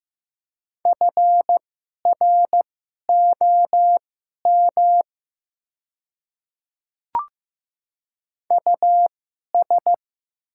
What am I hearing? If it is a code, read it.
Morse code, heard as a keyed tone, about 15 wpm, spaced as for 11 wpm, 700 Hz. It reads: FROM US
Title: 2x Character Spacing - Top 300 Words - 15wpm